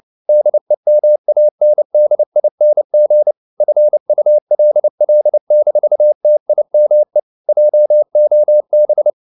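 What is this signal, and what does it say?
Morse code, heard as a keyed tone, about 29 wpm, 600 Hz. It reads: DEMANDING FULL-TIME JOB
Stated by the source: Morse code at 29 wpm, sending Ken Follett's novel Never in English, from CW med Gnister.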